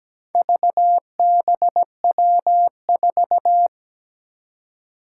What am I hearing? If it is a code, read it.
Morse code, heard as a keyed tone, about 17 wpm, 700 Hz. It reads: VBW4
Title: Four Challenging Letter-Number Combinations 17wpm